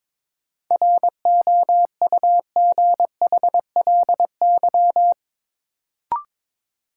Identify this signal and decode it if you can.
Morse code, heard as a keyed tone, about 22 wpm, 700 Hz. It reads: ROUGHLY
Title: Sentences from Top 1500 Words - New Word Drill Alt - 22wpm